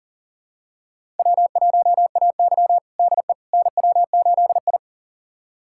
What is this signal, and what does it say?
Morse code, heard as a keyed tone, about 40 wpm, 700 Hz. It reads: W1AY DE NW8I
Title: USA Callsign DE Callsign - 4 Characters - 40wpm